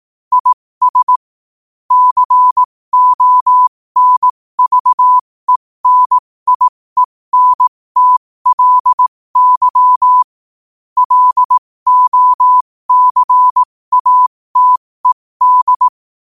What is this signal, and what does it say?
Morse code, heard as a keyed tone, about 18 wpm, spaced as for 15 wpm, 1000 Hz. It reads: IS CONVENIENTLY LOCATED
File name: qotdfiles / 15wpm